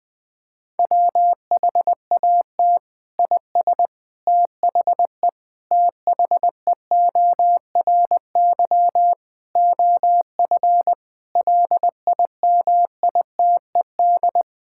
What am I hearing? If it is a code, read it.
Morse code, heard as a keyed tone, about 20 wpm, 700 Hz. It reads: WHAT IS THE THEORY OF LIMITED
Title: Sentences from Top 1000 Words - Review Part 6 - 20wpm